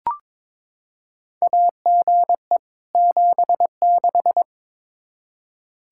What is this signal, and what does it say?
Morse code, heard as a keyed tone, about 22 wpm, 700 Hz. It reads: AGE 76
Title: K1USN SST QSO Practice 22wpm